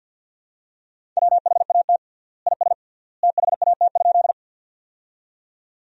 Morse code, heard as a keyed tone, about 52 wpm, 700 Hz. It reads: WHAT IS THAT?
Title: 3x Word-Spacing - Sentences from Top 100 Words - 50wpm